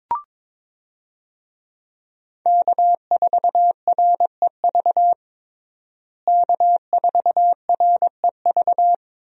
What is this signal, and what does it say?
Morse code, heard as a keyed tone, about 22 wpm, 700 Hz. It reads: K4REV K4REV
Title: USA Call Signs - 5 Characters - Repeated 2x - 22wpm